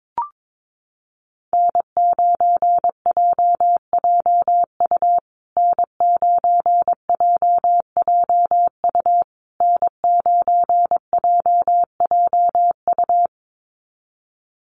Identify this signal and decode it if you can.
Morse code, heard as a keyed tone, about 22 wpm, 700 Hz. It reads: N9JJU N9JJU N9JJU
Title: USA Call Signs Repeated 3x 22wpm